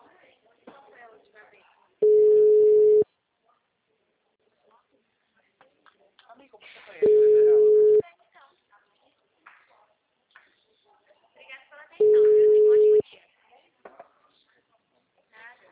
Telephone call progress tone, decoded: ringback tone

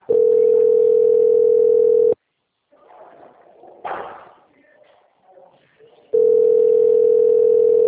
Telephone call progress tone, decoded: ringback tone

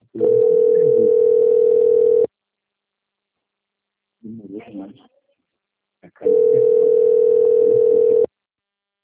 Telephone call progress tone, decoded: ringback tone

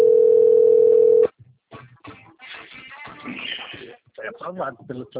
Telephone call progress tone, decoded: ringback tone

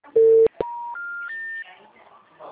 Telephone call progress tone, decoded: special information tone

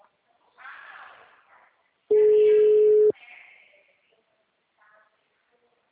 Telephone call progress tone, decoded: ringback tone

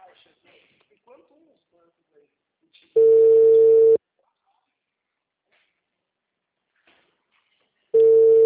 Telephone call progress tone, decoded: ringback tone

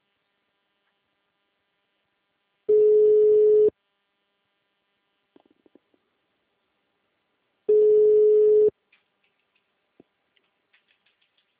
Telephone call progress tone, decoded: ringback tone

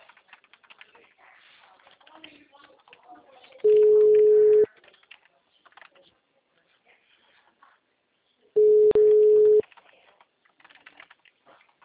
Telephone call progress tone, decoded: ringback tone

